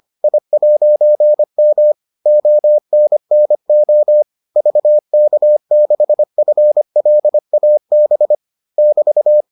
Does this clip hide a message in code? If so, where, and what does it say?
Yes: Morse code from 0.2 to 9.4 s, heard as a keyed tone, about 25 wpm, 600 Hz. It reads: I'M ONNO VK6FLAB <BT>